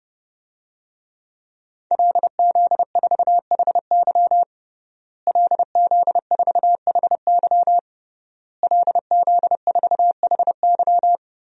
Morse code, heard as a keyed tone, about 32 wpm, 700 Hz. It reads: LZ4HY LZ4HY LZ4HY
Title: DXCC-100 Call Signs - 5 Characters - Repeated 3x - 30wpm